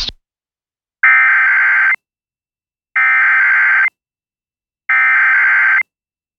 Emergency Alert System (SAME) tones (header bursts), heard as harsh buzzing bursts. It is heard from 1.0 to 5.8 s.